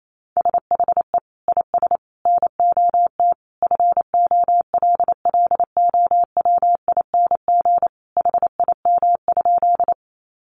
Morse code, heard as a keyed tone, about 28 wpm, 700 Hz. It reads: SHE IS NOT FOLLOWING HIM?
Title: Sentences from Top 700 Words 28wpm